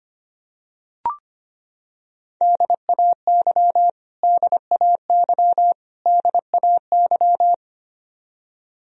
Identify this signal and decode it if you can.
Morse code, heard as a keyed tone, about 25 wpm, 700 Hz. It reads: DAY DAY DAY
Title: Top 100 Words Repeated 25wpm